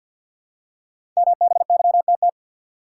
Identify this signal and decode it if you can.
Morse code, heard as a keyed tone, about 50 wpm, 700 Hz. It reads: MBYTT
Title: Code Groups - 5 Letter:Numbers - 50wpm